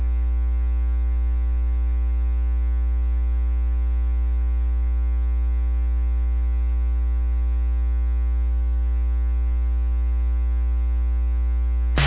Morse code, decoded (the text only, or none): none